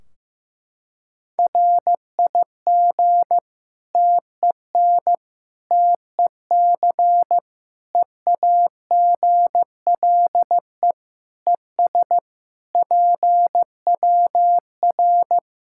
Morse code, heard as a keyed tone, about 15 wpm, 700 Hz. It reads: RIG TEN TEC EAGLE ES PWR
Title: K1USN SST QSO Practice 15wpm